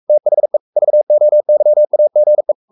Morse code, heard as a keyed tone, about 44 wpm, 600 Hz. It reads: THE VOYAGE
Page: Romanen Never af Ken Follet på engelsk.